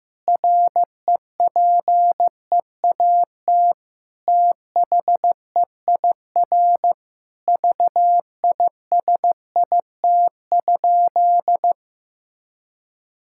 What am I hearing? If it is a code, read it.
Morse code, heard as a keyed tone, about 16 wpm, 700 Hz. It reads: REPEAT THEIR VISIT?
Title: Sentences from Top 1600 Words 15wpm